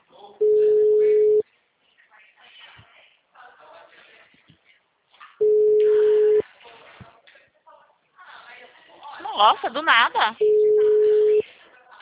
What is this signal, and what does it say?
Ringback tone (Europe, 425 Hz), with speech.